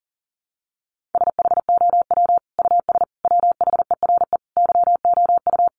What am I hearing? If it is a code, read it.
Morse code, heard as a keyed tone, about 40 wpm, 700 Hz. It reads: SHOW US WHERE YOU
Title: Sentences from Top 200 Words 40wpm